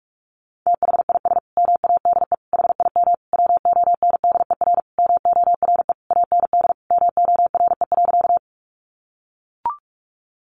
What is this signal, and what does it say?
Morse code, heard as a keyed tone, about 44 wpm, 700 Hz. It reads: THIS MADE HIM WONDER MORE AND MORE.
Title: Sentences from Top 500 Words 45wpm